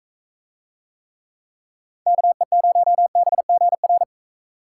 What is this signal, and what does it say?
Morse code, heard as a keyed tone, about 39 wpm, 700 Hz. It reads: KE0BGR